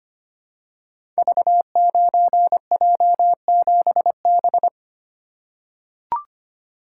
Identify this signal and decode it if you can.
Morse code, heard as a keyed tone, about 25 wpm, 700 Hz. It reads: V9J7B
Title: Code Groups - 5 Letter:Numbers - 25wpm